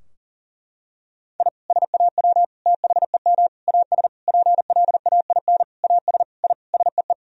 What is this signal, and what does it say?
Morse code, heard as a keyed tone, about 40 wpm, 700 Hz. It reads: I SAW THEM AS PLAIN AS I SEE
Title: Sentences from Top 1000 Words - Review Part 2 - 40wpm